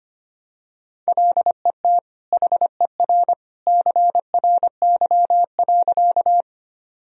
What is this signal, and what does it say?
Morse code, heard as a keyed tone, about 25 wpm, 700 Hz. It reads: LET HER CRY.
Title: Sentences from Top 1000 Words - Review Part 2 - 25wpm